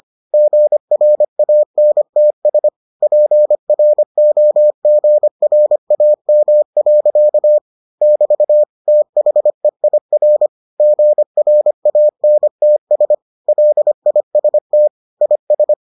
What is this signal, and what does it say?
Morse code, heard as a keyed tone, about 25 wpm, 600 Hz. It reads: GRANTS PROGRAM. <BT> THEIR GRANTS LIST IS